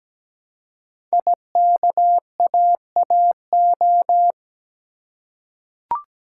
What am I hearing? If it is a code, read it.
Morse code, heard as a keyed tone, about 17 wpm, 700 Hz. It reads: IKAAO
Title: Code Groups - 5 Letters - 17wpm